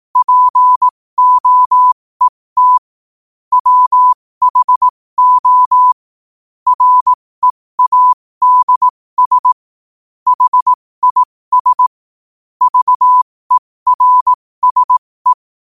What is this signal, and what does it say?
Morse code, heard as a keyed tone, about 18 wpm, spaced as for 15 wpm, 1000 Hz. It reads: POET WHO READS HIS VERSE
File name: qotdfiles / 15wpm